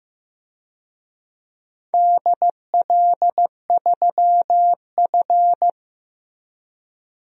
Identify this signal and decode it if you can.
Morse code, heard as a keyed tone, about 16 wpm, 700 Hz. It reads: DL3F
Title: DXCC-100 Call Signs - 4 Characters - Repeated 3x - 15wpm